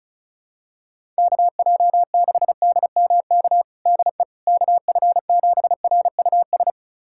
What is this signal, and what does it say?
Morse code, heard as a keyed tone, about 36 wpm, 700 Hz. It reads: KJ6DMK DE KF7RUS
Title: USA Callsign DE Callsign - 6 Characters - 35wpm